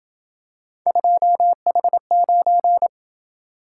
Morse code, heard as a keyed tone, about 27 wpm, 700 Hz. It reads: 2H9